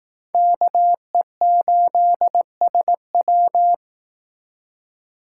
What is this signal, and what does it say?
Morse code, heard as a keyed tone, about 18 wpm, 700 Hz. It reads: KE8SW
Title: Speed Racing - USA Call Signs 15wpm